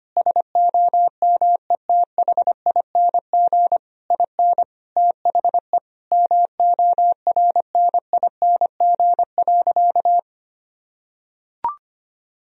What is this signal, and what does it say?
Morse code, heard as a keyed tone, about 25 wpm, 700 Hz. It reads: SOMETHING IN THE MORNING.
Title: Sentences from Top 1000 Words - Review Part 4 - 25wpm